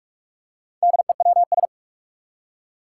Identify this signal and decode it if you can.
Morse code, heard as a keyed tone, about 43 wpm, 700 Hz. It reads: DEWS
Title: Four Challenging Letter-Number Combinations 45wpm